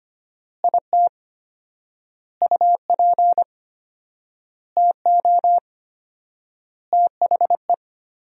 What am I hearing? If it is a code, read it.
Morse code, heard as a keyed tone, about 25 wpm, 700 Hz. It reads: IT UP TO THE
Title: 4x Word-Spacing - Sentences from Top 400 Words - 25wpm